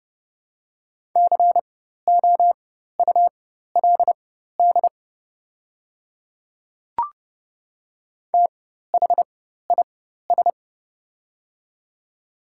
Morse code, heard as a keyed tone, about 30 wpm, spaced as for 14 wpm, 690 Hz. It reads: COULD THIS
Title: 4x Character Spacing - Top 100 Words - 30wpm